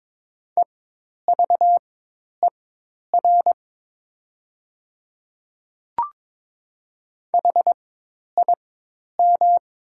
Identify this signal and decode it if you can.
Morse code, heard as a keyed tone, about 22 wpm, spaced as for 10 wpm, 700 Hz. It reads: EVER HIM
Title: 4x Character Spacing - Top 500 Words - 22wpm